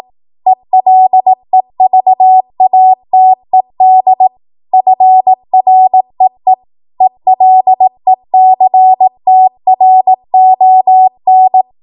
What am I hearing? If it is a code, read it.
Morse code, heard as a keyed tone, about 18 wpm, 750 Hz. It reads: ELEVATED FREE ELECTRON
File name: Morse / 18WPM